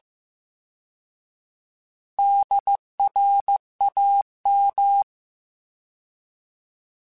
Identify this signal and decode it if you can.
Morse code, heard as a keyed tone, about 15 wpm, 800 Hz. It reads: DRAM
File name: Alphabetical / Alphabetical mp3s / Ds @13